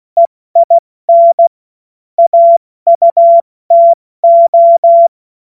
Morse code, heard as a keyed tone, about 16 wpm, spaced as for 14 wpm, 680 Hz. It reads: EIN AUTO
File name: Morsekurs / Witze / Witz 28. Dezember 2021, 70 F80